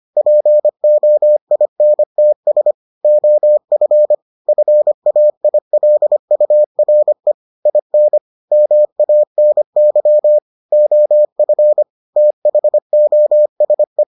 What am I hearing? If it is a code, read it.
Morse code, heard as a keyed tone, about 25 wpm, 600 Hz. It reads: POINTS OF FAILURE IN MANY OF THOSE